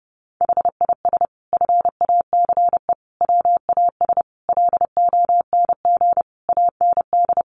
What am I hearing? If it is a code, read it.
Morse code, heard as a keyed tone, about 30 wpm, 700 Hz. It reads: HIS FACE WAS LONG AND
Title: Sentences from Top 1200 Words 30wpm